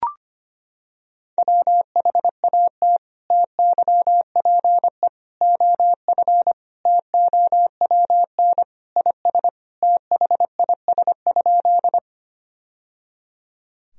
Morse code, heard as a keyed tone, about 25 wpm, 700 Hz. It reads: WHAT TYPE OF TOWN IS THIS?
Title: Sentences from Top 1000 Words - Review Part 3 - 25wpm